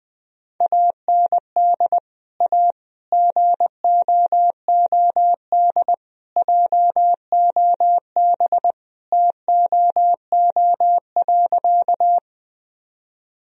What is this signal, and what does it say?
Morse code, heard as a keyed tone, about 20 wpm, 700 Hz. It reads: AND A GOOD JOB TOO.
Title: Sentences from Top 1000 Words - Review Part 3 - 20wpm